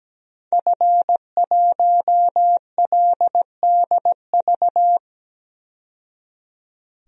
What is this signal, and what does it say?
Morse code, heard as a keyed tone, about 17 wpm, 700 Hz. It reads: F1LDV
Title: DXCC-100 Call Signs - 5 Characters - 17wpm